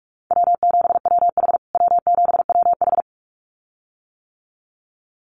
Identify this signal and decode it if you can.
Morse code, heard as a keyed tone, about 44 wpm, 700 Hz. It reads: W7WH W7WH
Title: USA Call Signs Repeated 2x 45wpm